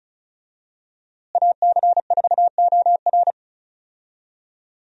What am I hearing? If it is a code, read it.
Morse code, heard as a keyed tone, about 36 wpm, 700 Hz. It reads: AC4OR
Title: USA Call Signs - 5 Characters - Repeated 3x - 35wpm